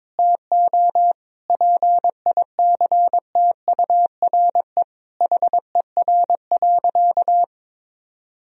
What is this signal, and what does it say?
Morse code, heard as a keyed tone, about 22 wpm, 700 Hz. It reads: TO PICTURE HER.
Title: Sentences from Top 1000 Words - Review Part 1 - 22wpm